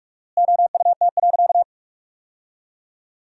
Morse code, heard as a keyed tone, about 42 wpm, 700 Hz. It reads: OUT.
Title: 4x Word-Spacing - Sentences from Top 500 Words - 45wpm